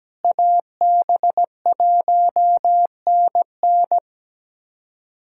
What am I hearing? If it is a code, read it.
Morse code, heard as a keyed tone, about 17 wpm, 700 Hz. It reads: AB1NN
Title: USA Callsign DE Callsign - 5 Characters - 17wpm